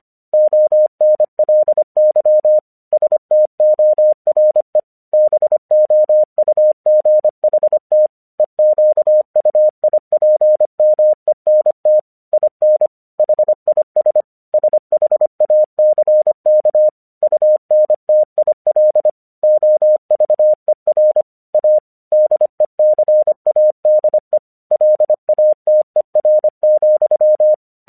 Morse code, heard as a keyed tone, about 25 wpm, 610 Hz. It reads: ONLY STORE BOUGHT EQUIPMENT IN HIS SHACK UNTIL OVER A DECADE LATER,
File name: 1 English 25wpm / AQC12